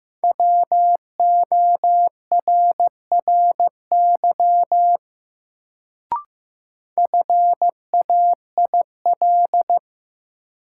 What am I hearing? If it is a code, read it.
Morse code, heard as a keyed tone, about 15 wpm, 700 Hz. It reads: WORRY FAIL
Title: Sentences from Top 1100 Words - New Word Drill - 15wpm